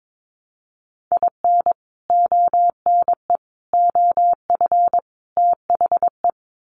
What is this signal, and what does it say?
Morse code, heard as a keyed tone, about 22 wpm, 700 Hz. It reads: IN ONE OF THE
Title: Sets of 4 Words 22wpm